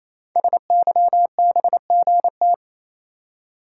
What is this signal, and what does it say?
Morse code, heard as a keyed tone, about 29 wpm, 700 Hz. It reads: SYBGT
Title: Code Groups - 5 Letters - 28wpm